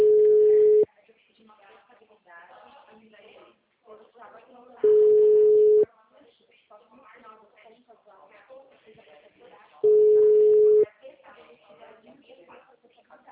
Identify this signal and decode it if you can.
Ringback tone (Europe, 425 Hz), with speech.